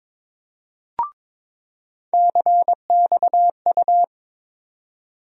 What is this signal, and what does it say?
Morse code, heard as a keyed tone, about 22 wpm, 700 Hz. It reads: CXU
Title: USA Call Sign Postfixes - 3 Letters 22wpm